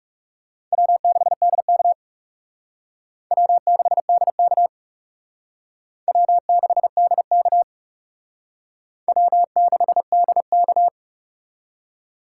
Morse code, heard as a keyed tone, about 35 wpm, 700 Hz. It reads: W6DK W6DK W6DK W6DK
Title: Speed Racing - USA 1x2 Call Signs 30wpm